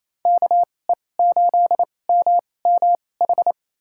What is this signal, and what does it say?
Morse code, heard as a keyed tone, about 28 wpm, spaced as for 20 wpm, 700 Hz. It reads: KE8MMH
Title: USA Call Signs - 6 Characters - 2x Character Spacing - 28wpm